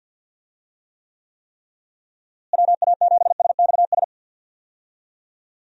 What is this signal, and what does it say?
Morse code, heard as a keyed tone, about 48 wpm, 700 Hz. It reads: WA7SXS
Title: USA Call Signs - 6 Characters - Repeated 3x - 50wpm